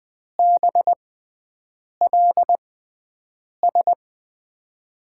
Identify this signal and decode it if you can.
Morse code, heard as a keyed tone, about 20 wpm, spaced as for 7 wpm, 700 Hz. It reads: BLS
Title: USA Call Signs - 6x Character Spacing - 20wpm